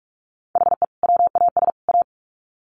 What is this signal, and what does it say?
Morse code, heard as a keyed tone, about 44 wpm, 700 Hz. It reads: HE WAS A
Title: Sets of 3 Words 45wpm